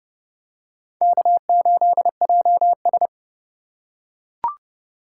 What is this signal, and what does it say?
Morse code, heard as a keyed tone, about 31 wpm, 700 Hz. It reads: K8JS